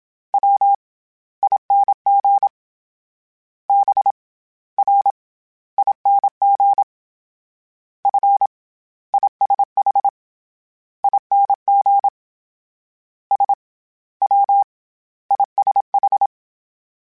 Morse code, heard as a keyed tone, about 27 wpm, 800 Hz. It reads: W ING B R ING F ISH ING S W ISH